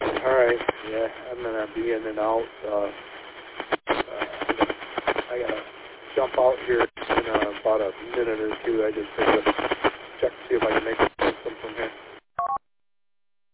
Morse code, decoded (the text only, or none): none